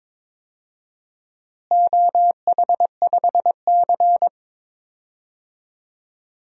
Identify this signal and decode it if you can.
Morse code, heard as a keyed tone, about 22 wpm, 700 Hz. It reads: OH5C
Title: DXCC-100 Call Signs - 4 Characters - Repeated 3x - 22wpm